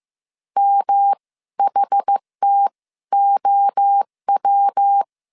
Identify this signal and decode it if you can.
Morse code, heard as a keyed tone, about 15 wpm, spaced as for 12 wpm, 800 Hz. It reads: MHTOW
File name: CW / DET / 1 3 1500C S A 14wpm 2447